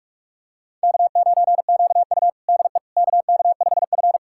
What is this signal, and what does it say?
Morse code, heard as a keyed tone, about 46 wpm, 700 Hz. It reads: K9QU DE KK5F
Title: USA Callsign DE Callsign - 4 Characters - 45wpm